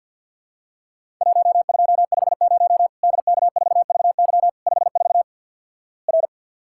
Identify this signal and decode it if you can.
Morse code, heard as a keyed tone, about 52 wpm, 700 Hz. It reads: 1250 DK4VY 54 R
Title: Worked All Europe DX Contest - QTC Exchange - 50wpm